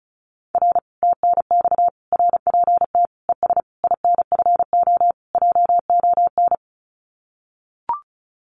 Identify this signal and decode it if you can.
Morse code, heard as a keyed tone, about 36 wpm, 700 Hz. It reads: R TNX RPT ES INFO JON
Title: Common QSO Phrases Encore 35wpm